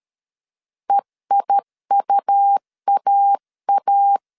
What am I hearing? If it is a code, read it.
Morse code, heard as a keyed tone, about 13 wpm, 800 Hz. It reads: EIUAA
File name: CW / DET / 1 3 1500C S A 12wpm 2905